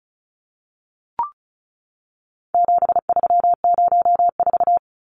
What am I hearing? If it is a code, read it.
Morse code, heard as a keyed tone, about 34 wpm, 700 Hz. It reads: 7304
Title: Numbers - 4-Digit 35wpm